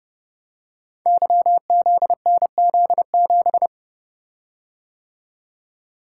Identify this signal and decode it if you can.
Morse code, heard as a keyed tone, about 30 wpm, 700 Hz. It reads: YZNZ7
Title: Code Groups - 5 Letter:Numbers - 30wpm